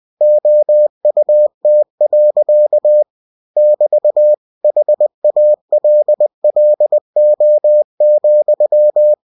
Morse code, heard as a keyed tone, about 20 wpm, 600 Hz. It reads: OUT. <BT> HALLO,
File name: cw-books / winnie-the-pooh / winnie-20wpm-page-0029